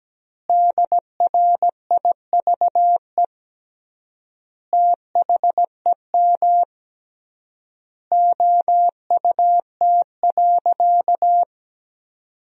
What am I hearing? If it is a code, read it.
Morse code, heard as a keyed tone, about 17 wpm, 700 Hz. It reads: DRIVE THEM OUT.
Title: 3x Word-Spacing - Sentences from Top 500 Words - 17wpm